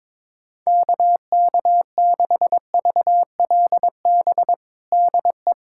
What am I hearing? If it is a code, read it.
Morse code, heard as a keyed tone, about 22 wpm, 700 Hz. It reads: KK6VLB DE
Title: USA Callsign DE Callsign - 6 Characters - 22wpm